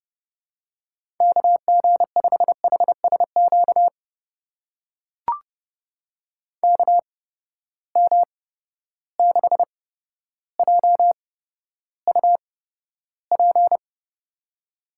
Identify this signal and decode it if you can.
Morse code, heard as a keyed tone, about 30 wpm, spaced as for 13 wpm, 700 Hz. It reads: KG5HSQ KM6JUP
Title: USA Call Signs - 6 Characters - 8x Character Spacing - 30wpm